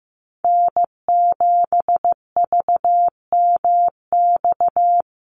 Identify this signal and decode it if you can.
Morse code, heard as a keyed tone, about 15 wpm, 700 Hz. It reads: N7VMX
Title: USA Call Signs 15wpm